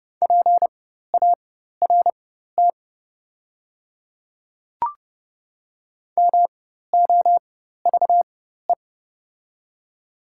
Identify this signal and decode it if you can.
Morse code, heard as a keyed tone, about 30 wpm, spaced as for 14 wpm, 700 Hz. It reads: PART MOVE
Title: 4x Character Spacing - Top 200 Words - 30wpm